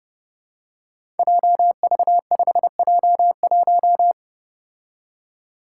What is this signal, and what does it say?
Morse code, heard as a keyed tone, about 30 wpm, 700 Hz. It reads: JV5J1
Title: Five Challenging Letter-Number Combinations 30wpm